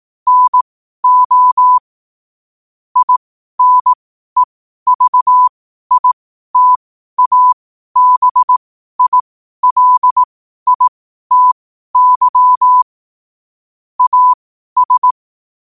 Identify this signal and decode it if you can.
Morse code, heard as a keyed tone, about 18 wpm, spaced as for 12 wpm, 1000 Hz. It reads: NO INEVITABILITY AS
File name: qotdfiles / 12wpm